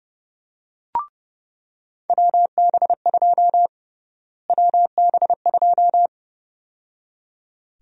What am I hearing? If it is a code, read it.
Morse code, heard as a keyed tone, about 30 wpm, 700 Hz. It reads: WB2 WB2
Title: USA Call Sign Prefixes - Repeated 2x - 30wpm